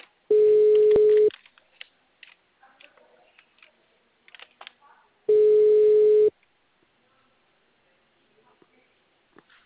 Ringback tone (Europe, 425 Hz).